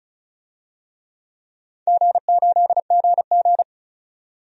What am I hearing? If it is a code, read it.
Morse code, heard as a keyed tone, about 36 wpm, 700 Hz. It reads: G8GG
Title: DXCC-100 Call Signs - 4 Characters - 35wpm